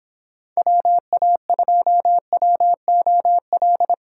Morse code, heard as a keyed tone, about 28 wpm, 700 Hz. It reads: WA2WOL